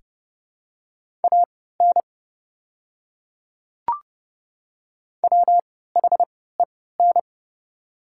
Morse code, heard as a keyed tone, about 30 wpm, spaced as for 17 wpm, 700 Hz. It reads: AN WHEN
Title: 3x Character Spacing - Top 300 Words - 30wpm